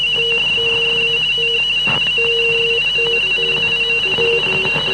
RTTY, about 50 baud. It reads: ODESSA TH N MAKSIMOW ZIMINU=
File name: VTP 8645 1628